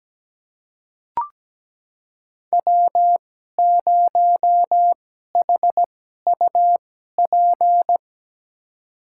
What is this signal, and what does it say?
Morse code, heard as a keyed tone, about 17 wpm, spaced as for 12 wpm, 700 Hz. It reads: W0HUP
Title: USA Call Signs - 5 Characters - 2x Character Spacing - 17wpm